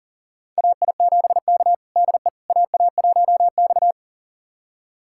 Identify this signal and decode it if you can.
Morse code, heard as a keyed tone, about 37 wpm, 700 Hz. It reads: AI7K DE AA1X
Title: USA Callsign DE Callsign - 4 Characters - 40wpm